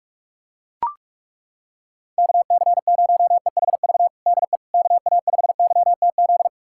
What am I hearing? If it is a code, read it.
Morse code, heard as a keyed tone, about 41 wpm, 700 Hz. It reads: KC0EHV DE KA5YTZ
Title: USA Callsign DE Callsign - 6 Characters - 45wpm